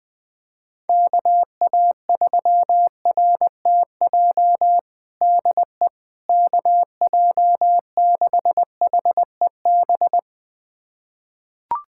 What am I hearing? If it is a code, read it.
Morse code, heard as a keyed tone, about 20 wpm, 700 Hz. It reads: KA3RTJ DE KJ6HEB
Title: USA Callsign DE Callsign 20wpm